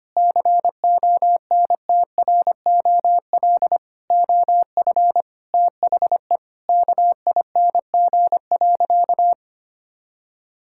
Morse code, heard as a keyed tone, about 25 wpm, 700 Hz. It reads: CONTROL OF THE KING.
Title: Sentences from Top 1000 Words - Review Part 5 - 25wpm